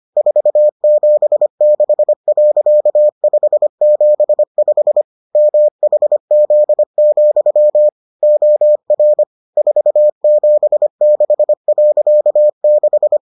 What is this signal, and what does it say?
Morse code, heard as a keyed tone, about 25 wpm, 600 Hz. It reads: 476.575 MHZ, OR 476.6